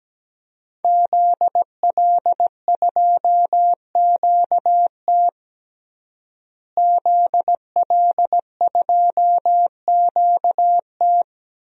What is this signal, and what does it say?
Morse code, heard as a keyed tone, about 17 wpm, 700 Hz. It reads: ZL2QT ZL2QT
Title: DXCC-100 Call Signs - 5 Characters - Repeated 3x - 17wpm